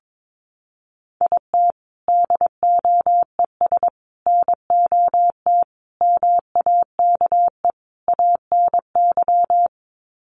Morse code, heard as a keyed tone, about 22 wpm, 700 Hz. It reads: IT DOES NOT MAKE ANY
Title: Sets of 5 Words Encore2 22wpm